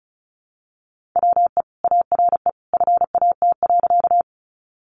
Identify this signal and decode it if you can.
Morse code, heard as a keyed tone, about 34 wpm, 700 Hz. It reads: WE ARE FAT.
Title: Sentences from Top 600 Words 35wpm